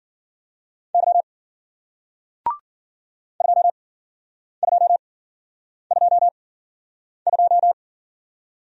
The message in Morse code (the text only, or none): Y 2222